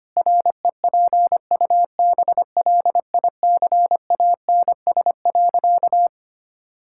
Morse code, heard as a keyed tone, about 25 wpm, 700 Hz. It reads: REPUBLICANS.
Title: Pangram Challenge v2 - 25wpm